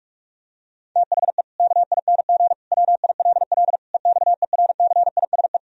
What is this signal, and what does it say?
Morse code, heard as a keyed tone, about 46 wpm, 700 Hz. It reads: THE KING WILL EXERCISE
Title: Sentences from Top 1000 Words - Review Part 4 - 45wpm